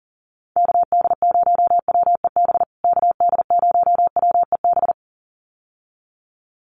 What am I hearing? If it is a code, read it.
Morse code, heard as a keyed tone, about 37 wpm, 700 Hz. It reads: KD0WEB KD0WEB
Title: USA Call Signs Repeated 2x 40wpm